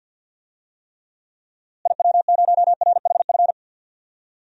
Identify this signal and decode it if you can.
Morse code, heard as a keyed tone, about 51 wpm, 700 Hz. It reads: IW0RHF
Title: DXCC-100 Call Signs - 6 Characters - Repeated 3x - 50wpm